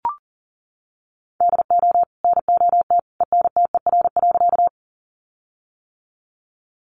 Morse code, heard as a keyed tone, about 40 wpm, 700 Hz. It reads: DO NOT ENTER.
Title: Sentences from Top 1300 Words 40wpm